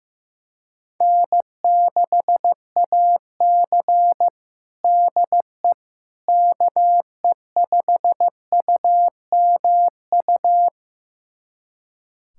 Morse code, heard as a keyed tone, about 16 wpm, 700 Hz. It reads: N6AC DE KE5UMU
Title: USA Callsign DE Callsign 15wpm